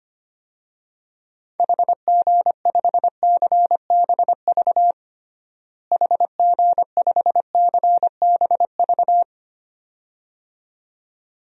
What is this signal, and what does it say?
Morse code, heard as a keyed tone, about 25 wpm, 700 Hz. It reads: HG5CBV HG5CBV
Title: DXCC-100 Call Signs - 6 Characters - Repeated 2x - 25wpm